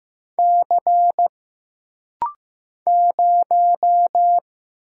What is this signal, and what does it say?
Morse code, heard as a keyed tone, about 15 wpm, 700 Hz. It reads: C 0
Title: Single Letter-Number Alt - Rapid-Fire 15wpm